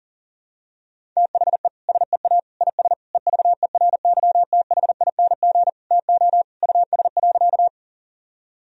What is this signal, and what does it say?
Morse code, heard as a keyed tone, about 40 wpm, 700 Hz. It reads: THE SEA IS EVERYTHING TO US.
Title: Sentences from Top 1000 Words - Review Part 1 - 40wpm